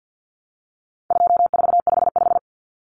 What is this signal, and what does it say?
Morse code, heard as a keyed tone, about 50 wpm, 700 Hz. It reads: 2455
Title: Common Cut Numbers - 4-Digit 50wpm